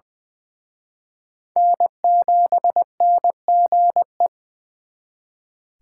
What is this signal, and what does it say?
Morse code, heard as a keyed tone, about 24 wpm, 700 Hz. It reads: N7NGE
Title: Speed Racing - USA 1x3 Call Signs 20wpm